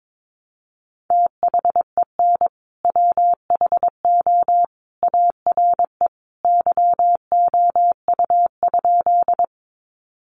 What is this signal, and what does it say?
Morse code, heard as a keyed tone, about 22 wpm, 700 Hz. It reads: THEN WHO ARE YOU?
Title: Sentences from Top 100 Words 22wpm